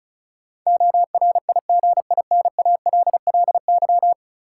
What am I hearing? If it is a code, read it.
Morse code, heard as a keyed tone, about 36 wpm, 700 Hz. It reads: ORIGINALLY
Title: Sentences from Top 1500 Words - New Word Drill - 35wpm